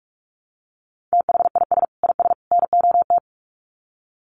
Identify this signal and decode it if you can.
Morse code, heard as a keyed tone, about 44 wpm, 700 Hz. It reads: THIS IS NOT
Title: Sets of 3 Words 45wpm